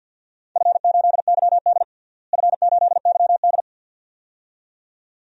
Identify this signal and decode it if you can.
Morse code, heard as a keyed tone, about 47 wpm, 700 Hz. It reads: F8YD F8YD
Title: DXCC-100 Call Signs - 4 Characters - Repeated 2x - 50wpm